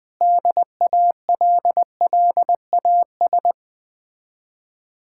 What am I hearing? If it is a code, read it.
Morse code, heard as a keyed tone, about 20 wpm, 700 Hz. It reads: DALLAS
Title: Speed-Racing Common QSO Phrases - 17wpm